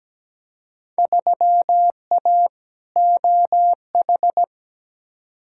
Morse code, heard as a keyed tone, about 17 wpm, 700 Hz. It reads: 3A OH
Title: Field Day Exchanges - 17wpm